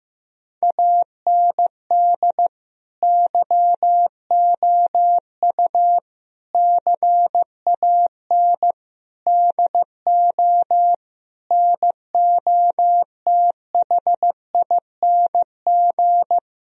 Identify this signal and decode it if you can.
Morse code, heard as a keyed tone, about 16 wpm, 700 Hz. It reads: AND YOU CAN DO NOTHING
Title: Sentences from Top 1000 Words - Review Part 1 - 15wpm